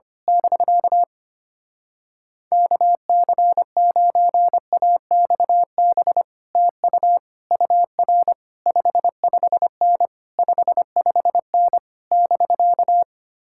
Morse code, heard as a keyed tone, about 24 wpm, 700 Hz. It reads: <BK> KC9AXB TU UR 55N 55N <BK>
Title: Speed-Racing Common QSO Phrases - 25wpm